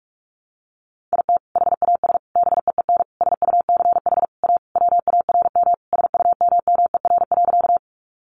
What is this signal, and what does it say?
Morse code, heard as a keyed tone, about 44 wpm, 700 Hz. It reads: IT HAS BEEN SUCH A WARM SUMMER.
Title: Sentences from Top 1100 Words 45wpm